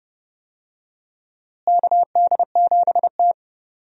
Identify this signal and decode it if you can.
Morse code, heard as a keyed tone, about 30 wpm, 700 Hz. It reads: KD7T
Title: USA Call Signs - 4 Characters - Repeated 2x - 30wpm